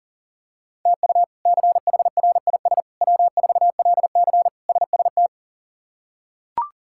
Morse code, heard as a keyed tone, about 40 wpm, 700 Hz. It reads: TU CHRIS W4LC SST
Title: K1USN SST QSO Practice - 40wpm